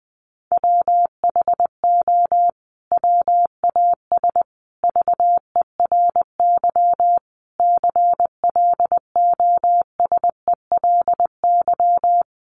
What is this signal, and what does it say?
Morse code, heard as a keyed tone, about 20 wpm, 700 Hz. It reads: WHO WAS VERY CLOSELY
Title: Sentences from Top 1400 Words 20wpm